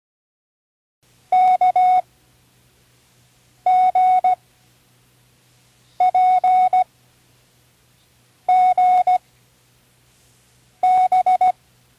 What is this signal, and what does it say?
Morse code, heard as a keyed tone, about 16 wpm, 740 Hz. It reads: K G P G B
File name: CWResources / Learning Morse Code (from Tune in the World) / 13 random